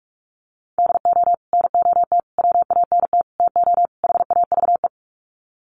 Morse code, heard as a keyed tone, about 44 wpm, 700 Hz. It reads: DO NOT WANT TO HAVE
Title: Sets of 5 Words 45wpm